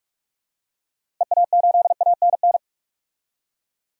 Morse code, heard as a keyed tone, about 43 wpm, 700 Hz. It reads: EA8ANN
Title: DXCC-100 Call Signs - 6 Characters - Repeated 2x - 45wpm